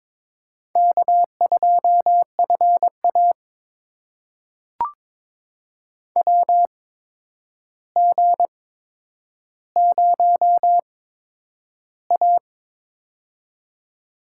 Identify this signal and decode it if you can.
Morse code, heard as a keyed tone, about 22 wpm, spaced as for 9 wpm, 700 Hz. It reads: K2FA WG0A
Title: USA Call Signs - 4 Characters - 8x Character Spacing - 22wpm